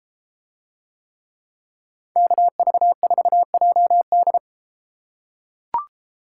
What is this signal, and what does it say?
Morse code, heard as a keyed tone, about 32 wpm, 700 Hz. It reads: KV4JD